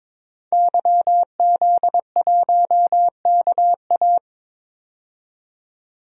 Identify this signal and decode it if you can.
Morse code, heard as a keyed tone, about 22 wpm, 700 Hz. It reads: YZ1KA
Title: DXCC-100 Call Signs - 5 Characters - 22wpm